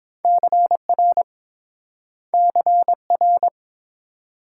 Morse code, heard as a keyed tone, about 30 wpm, 700 Hz. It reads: CR CR